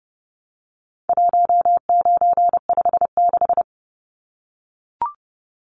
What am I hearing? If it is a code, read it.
Morse code, heard as a keyed tone, about 30 wpm, 700 Hz. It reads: 1956
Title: Numbers - 4-Digit 30wpm